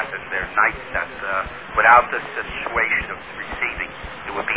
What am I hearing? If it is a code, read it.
Emergency Alert System (SAME) tones, heard as harsh buzzing bursts, with speech. It is heard from 0.7 to 3.9 s.